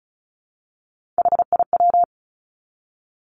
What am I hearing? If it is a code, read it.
Morse code, heard as a keyed tone, about 36 wpm, 700 Hz. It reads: HIW
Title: Three Challenging Letter-Number Combinations 35wpm